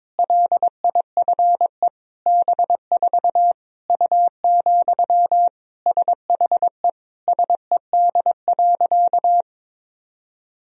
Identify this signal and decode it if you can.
Morse code, heard as a keyed tone, about 22 wpm, 700 Hz. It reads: LIFE B4 U, SHE SED.
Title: Sentences from Top 400 Words and Abbreviations 22wpm